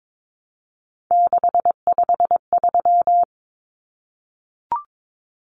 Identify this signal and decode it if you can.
Morse code, heard as a keyed tone, about 22 wpm, 700 Hz. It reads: 653
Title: Numbers - 3-Digit 22wpm